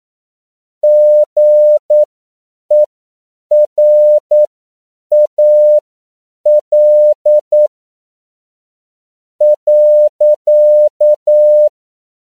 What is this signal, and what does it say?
Morse code, heard as a keyed tone, about 9 wpm, spaced as for 6 wpm, 600 Hz. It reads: GERAL.
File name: CW / 10PPM